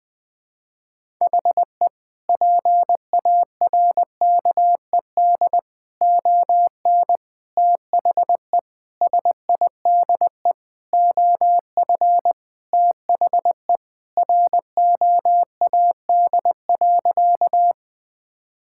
Morse code, heard as a keyed tone, about 21 wpm, 700 Hz. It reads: HE PARKED ON THE SIDE OF THE ROAD.
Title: Sentences with 5-Word Phrases - Spoken Beginning - 20wpm